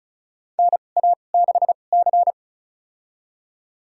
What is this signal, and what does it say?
Morse code, heard as a keyed tone, about 36 wpm, spaced as for 25 wpm, 700 Hz. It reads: NA6C
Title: USA Call Signs - 4 Characters - 2x Character Spacing - 35wpm